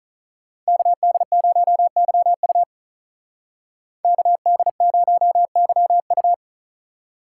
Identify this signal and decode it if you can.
Morse code, heard as a keyed tone, about 46 wpm, 700 Hz. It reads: KD0YU KD0YU